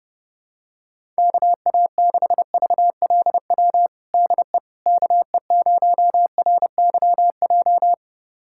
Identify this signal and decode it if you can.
Morse code, heard as a keyed tone, about 30 wpm, 700 Hz. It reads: KA6VLW DE KE0RYJ
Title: USA Callsign DE Callsign - 6 Characters - 30wpm